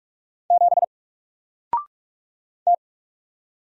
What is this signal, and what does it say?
Morse code, heard as a keyed tone, about 45 wpm, 700 Hz. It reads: 7 T
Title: Single Character (Letter:Number:Punctuation:Prosign) Alt 45wpm